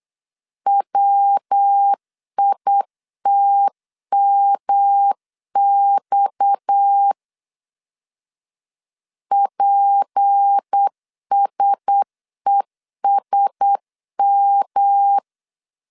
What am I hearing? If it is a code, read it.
Morse code, heard as a keyed tone, about 8 wpm, 800 Hz. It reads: WITMX PSESM
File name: LEZIONI / 6 PKX0 / 6 PKX 8wpm 3041